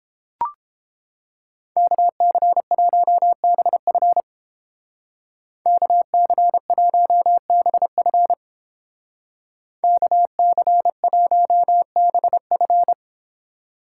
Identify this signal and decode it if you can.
Morse code, heard as a keyed tone, about 29 wpm, 700 Hz. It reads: KC1BF KC1BF KC1BF